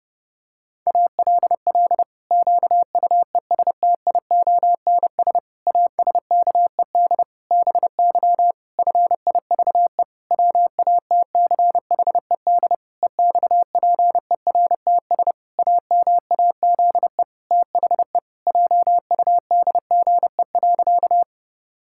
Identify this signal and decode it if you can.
Morse code, heard as a keyed tone, about 30 wpm, 700 Hz. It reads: ALL QUESTIONS ASKED BY FIVE WATCHED EXPERTS AMAZE THE JUDGE.
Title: Pangram Challenge v2 - 30wpm